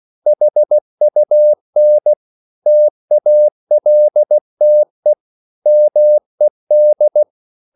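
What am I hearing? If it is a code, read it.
Morse code, heard as a keyed tone, about 16 wpm, 600 Hz. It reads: HUN TALTE MED
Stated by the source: Slow (QRS) Morse code at 16 wpm, from CW med Gnister.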